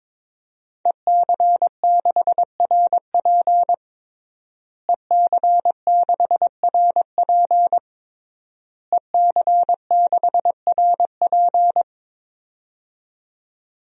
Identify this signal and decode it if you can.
Morse code, heard as a keyed tone, about 22 wpm, 700 Hz. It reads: EC6RP EC6RP EC6RP
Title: DXCC-100 Call Signs - 5 Characters - Repeated 3x - 22wpm